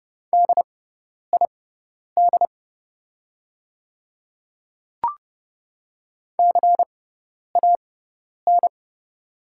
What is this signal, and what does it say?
Morse code, heard as a keyed tone, about 30 wpm, spaced as for 10 wpm, 700 Hz. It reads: DID CAN
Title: 6x Character Spacing - Top 100 Words - 30wpm